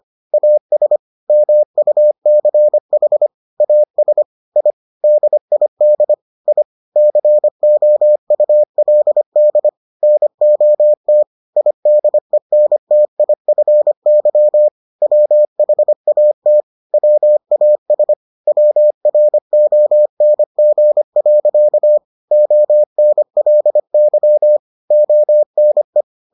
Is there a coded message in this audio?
Yes: Morse code from 0.3 to 26.0 s, heard as a keyed tone, about 25 wpm, 600 Hz. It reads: AS MUCH AS I DID I COULD NOT IDENTIFY WHAT WAS WRONG. ONLY ONE